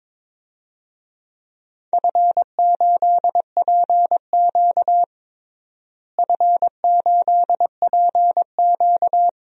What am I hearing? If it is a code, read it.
Morse code, heard as a keyed tone, about 22 wpm, 700 Hz. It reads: F8PQ F8PQ
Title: DXCC-100 Call Signs - 4 Characters - Repeated 3x - 22wpm